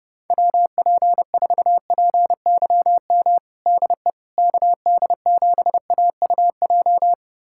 Morse code, heard as a keyed tone, about 30 wpm, 700 Hz. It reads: WP4PYM DE KD7AUJ
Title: USA Callsign DE Callsign - 6 Characters - 30wpm